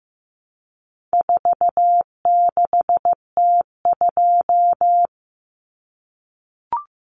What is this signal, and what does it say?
Morse code, heard as a keyed tone, about 15 wpm, 700 Hz. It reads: 46T2
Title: Common Cut Numbers - 4-Digit 15wpm